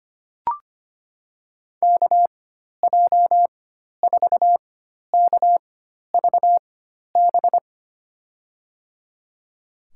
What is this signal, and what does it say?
Morse code, heard as a keyed tone, about 25 wpm, spaced as for 12 wpm, 700 Hz. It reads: KJ4KVB
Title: USA Call Signs - 6 Characters - 4x Character Spacing - 25wpm